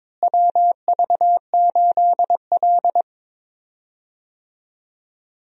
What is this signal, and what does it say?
Morse code, heard as a keyed tone, about 22 wpm, 700 Hz. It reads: WV8L
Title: USA Call Signs - 4 Characters - Repeated 2x - 22wpm